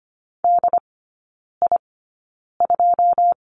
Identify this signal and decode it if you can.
Morse code, heard as a keyed tone, about 25 wpm, spaced as for 9 wpm, 700 Hz. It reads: DI2
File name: course / 25:10wpm / 093 Lesson 16e - Letter c - Three Characters 25wpm